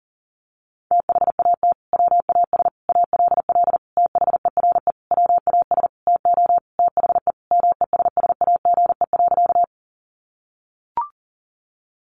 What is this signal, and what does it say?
Morse code, heard as a keyed tone, about 40 wpm, 700 Hz. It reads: THAT WAS ALL THERE WAS TO THE MESSAGE.
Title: Sentences from Top 1100 Words 40wpm